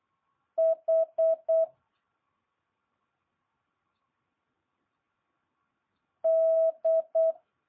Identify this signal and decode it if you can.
Morse code, heard as a keyed tone, about 8 wpm, 650 Hz. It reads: H D
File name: jh8cbh